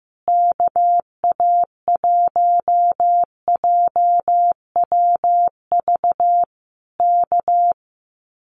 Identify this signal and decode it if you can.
Morse code, heard as a keyed tone, about 15 wpm, 700 Hz. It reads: KA1JWV K
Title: Common QSO Phrases Encore 15wpm